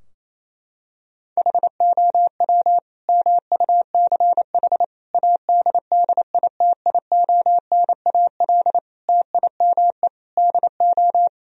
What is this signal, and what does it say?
Morse code, heard as a keyed tone, about 28 wpm, 700 Hz. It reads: HOW MUCH ADDITIONAL TIME DO
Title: Sentences from Top 1000 Words - Review Part 4 - 28wpm